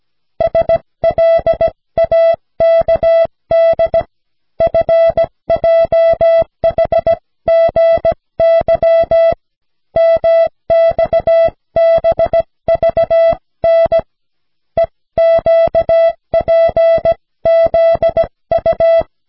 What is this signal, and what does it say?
Morse code, heard as a keyed tone, about 17 wpm, 660 Hz. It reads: SLAKD FJHGY MXBVN EQPZU
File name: cw / snelheid 4